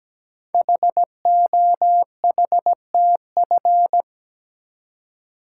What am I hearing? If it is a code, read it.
Morse code, heard as a keyed tone, about 17 wpm, 700 Hz. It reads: HOHTF
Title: Code Groups - 5 Letters - 17wpm